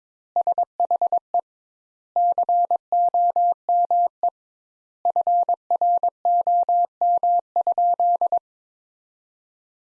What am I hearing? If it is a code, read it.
Morse code, heard as a keyed tone, about 22 wpm, 700 Hz. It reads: SHE COME FROM?
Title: Sentences Top 200 Words - 22wpm